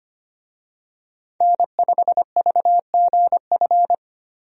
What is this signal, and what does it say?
Morse code, heard as a keyed tone, about 25 wpm, 700 Hz. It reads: N5VGF
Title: USA Call Signs - 5 Characters - Repeated 3x - 25wpm